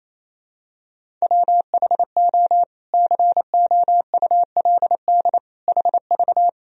Morse code, heard as a keyed tone, about 28 wpm, 700 Hz. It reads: WHO COULD HV
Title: Sentences from Top 300 Words and Abbreviations 28wpm